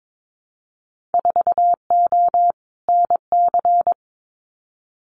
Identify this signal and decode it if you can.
Morse code, heard as a keyed tone, about 22 wpm, 700 Hz. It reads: 4O NC
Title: Winter Field Day Exchanges Encore 22wpm